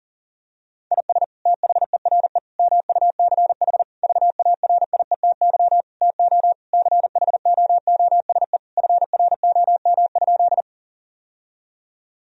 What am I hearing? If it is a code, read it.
Morse code, heard as a keyed tone, about 37 wpm, 700 Hz. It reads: IS THERE MUCH VARIETY TO CHOOSE FROM?
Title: Sentences from Top 1000 Words - Review Part 4 - 40wpm